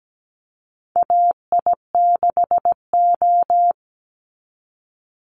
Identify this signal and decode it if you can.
Morse code, heard as a keyed tone, about 17 wpm, 700 Hz. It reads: AI6O
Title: Call Signs Repeated 2x - 17wpm